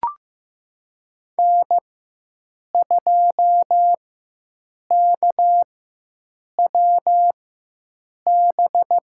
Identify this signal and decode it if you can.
Morse code, heard as a keyed tone, about 15 wpm, spaced as for 7 wpm, 700 Hz. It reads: N2KWB
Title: USA Call Signs - 4x Character Spacing - 15wpm